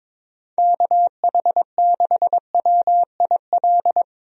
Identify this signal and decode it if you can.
Morse code, heard as a keyed tone, about 22 wpm, 700 Hz. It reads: KH6WIL
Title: USA Call Signs - 6 Characters - Repeated 3x - 22wpm